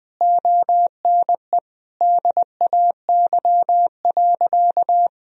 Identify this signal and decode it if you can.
Morse code, heard as a keyed tone, about 19 wpm, 700 Hz. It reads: ONE DAY.
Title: Sentences from Top 1000 Words - Review Part 4 - 20wpm